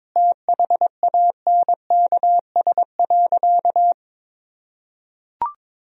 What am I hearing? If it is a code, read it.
Morse code, heard as a keyed tone, about 22 wpm, 700 Hz. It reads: THANKS.
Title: Sentences from Top 1000 Words - Review Part 3 - 22wpm